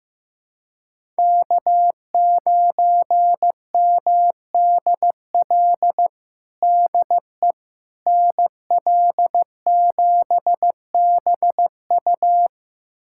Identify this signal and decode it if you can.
Morse code, heard as a keyed tone, about 15 wpm, 700 Hz. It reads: K9MDL DE NL7BU
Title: USA Callsign DE Callsign - 5 Characters - 15wpm